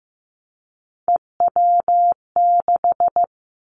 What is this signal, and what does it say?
Morse code, heard as a keyed tone, about 15 wpm, 700 Hz. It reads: EW6
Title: Callsign Prefixes for Top 50 DXCC - 15wpm